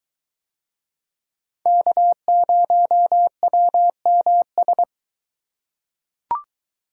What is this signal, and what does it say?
Morse code, heard as a keyed tone, about 21 wpm, 700 Hz. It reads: K0WMS